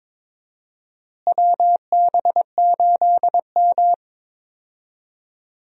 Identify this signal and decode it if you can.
Morse code, heard as a keyed tone, about 22 wpm, 700 Hz. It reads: WB8M
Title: USA Call Signs - 4 Characters - Repeated 3x - 22wpm